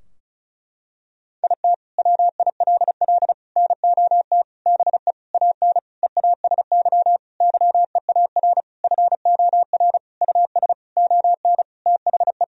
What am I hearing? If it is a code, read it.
Morse code, heard as a keyed tone, about 34 wpm, 700 Hz. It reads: IT WILL NOT BE AN EASY YEAR FOR US ON THE
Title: Sentences from Top 1000 Words - Review Part 2 - 35wpm